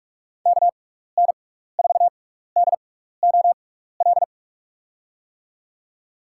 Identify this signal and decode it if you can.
Morse code, heard as a keyed tone, about 46 wpm, spaced as for 15 wpm, 700 Hz. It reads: KN4DOL
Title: USA Call Signs - 6 Characters - 6x Character Spacing - 45wpm